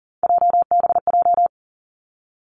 Morse code, heard as a keyed tone, about 39 wpm, 700 Hz. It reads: JBJ
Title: Three Challenging Letter-Number Combinations 40wpm